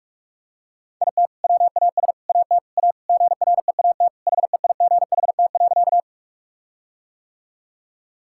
Morse code, heard as a keyed tone, about 46 wpm, 700 Hz. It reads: IT WAS AT A GREAT HEIGHT.
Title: Sentences from Top 1600 Words 45wpm